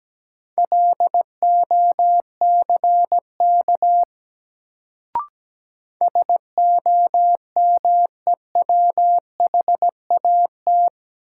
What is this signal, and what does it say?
Morse code, heard as a keyed tone, about 17 wpm, 700 Hz. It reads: LOCK SOMEWHAT
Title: Sentences from Top 1300 Words - New Word Drill Alt - 17wpm